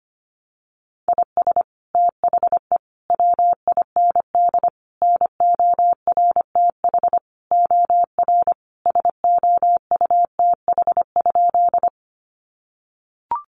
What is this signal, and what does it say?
Morse code, heard as a keyed tone, about 25 wpm, 700 Hz. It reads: IS THE WIND NORTH OR SOUTH?
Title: Sentences from Top 400 Words 25wpm